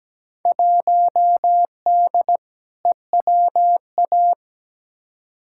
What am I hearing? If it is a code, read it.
Morse code, heard as a keyed tone, about 17 wpm, 700 Hz. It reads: 1D EWA
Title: Field Day Exchanges - 17wpm